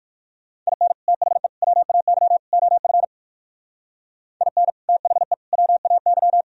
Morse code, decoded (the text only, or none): IN THE WAY OF IN THE WAY